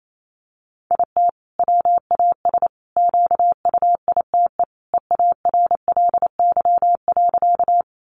Morse code, heard as a keyed tone, about 28 wpm, 700 Hz. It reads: IT WAS QUITE EARLY.
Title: Sentences from Top 700 Words 28wpm